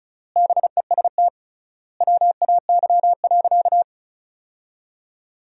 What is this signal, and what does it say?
Morse code, heard as a keyed tone, about 34 wpm, 700 Hz. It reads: BEST WAY.
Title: 3x Word-Spacing - Sentences from Top 400 Words - 35wpm